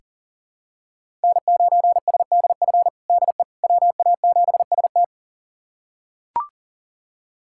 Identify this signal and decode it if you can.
Morse code, heard as a keyed tone, about 40 wpm, 700 Hz. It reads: N9SDF DE WA7ST
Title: USA Callsign DE Callsign - 5 Characters - 40wpm